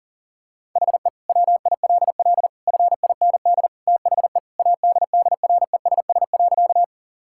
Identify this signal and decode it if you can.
Morse code, heard as a keyed tone, about 40 wpm, 700 Hz. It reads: HE WILL FIND THE ADDRESS.
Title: Sentences from Top 1000 Words - Review Part 6 - 40wpm